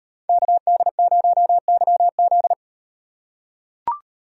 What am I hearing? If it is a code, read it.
Morse code, heard as a keyed tone, about 38 wpm, 700 Hz. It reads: KD0YZ